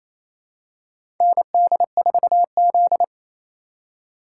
Morse code, heard as a keyed tone, about 28 wpm, 700 Hz. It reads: ND4Z
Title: USA Call Signs - 4x Character Spacing - 28wpm